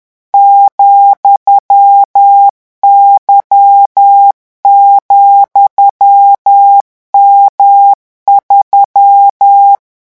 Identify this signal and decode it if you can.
Morse code, heard as a keyed tone, about 10 wpm, 800 Hz. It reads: ,Y,M3
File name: morse mp3 / revision13-vitesse 10 mots